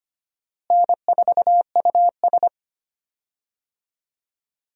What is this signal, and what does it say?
Morse code, heard as a keyed tone, about 25 wpm, 700 Hz. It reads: N4US
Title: USA Call Signs - 4 Characters - Repeated 3x - 25wpm